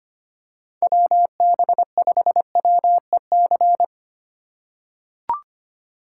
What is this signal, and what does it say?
Morse code, heard as a keyed tone, about 25 wpm, 700 Hz. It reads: WB5WEC
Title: K1USN SST QSO Practice - 25wpm